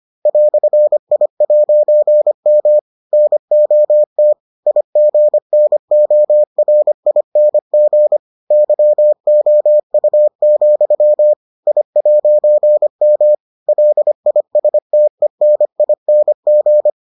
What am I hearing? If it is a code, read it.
Morse code, heard as a keyed tone, about 25 wpm, 600 Hz. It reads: "I'M NOT IGNORING YOU, I'M LISTENING